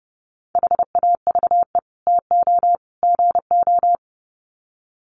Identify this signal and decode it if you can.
Morse code, heard as a keyed tone, about 30 wpm, 700 Hz. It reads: HAVE TO GO
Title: Sets of 3 Words 30wpm